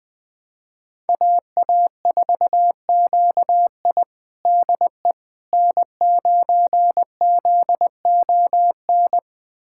Morse code, heard as a keyed tone, about 20 wpm, 700 Hz. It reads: AA4QI DE N9ZON
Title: USA Callsign DE Callsign - 5 Characters - 20wpm